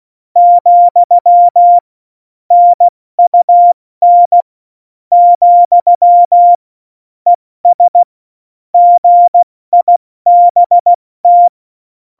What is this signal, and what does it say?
Morse code, heard as a keyed tone, about 16 wpm, 700 Hz. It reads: , NUN , ES GIBT
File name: Morsekurs / Witze / Witz 14 Juli 2020, 70 F80